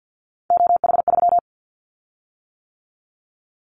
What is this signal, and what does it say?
Morse code, heard as a keyed tone, about 49 wpm, 700 Hz. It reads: OH3
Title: Callsign Prefixes for Top 25 DXCC - 50wpm